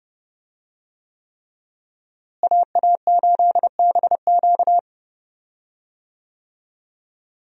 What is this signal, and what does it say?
Morse code, heard as a keyed tone, about 30 wpm, 700 Hz. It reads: AA8BQ
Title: USA Call Signs - 5 Characters - Repeated 3x - 30wpm